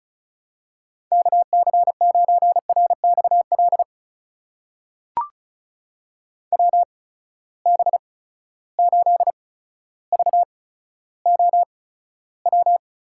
Morse code, heard as a keyed tone, about 36 wpm, spaced as for 15 wpm, 700 Hz. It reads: KC9RXL WB8VOW
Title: USA Call Signs - 6 Characters - 8x Character Spacing - 35wpm